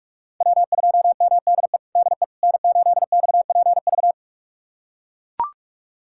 Morse code, heard as a keyed tone, about 43 wpm, 700 Hz. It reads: W2MDE DE N8XPV
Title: USA Callsign DE Callsign - 5 Characters - 45wpm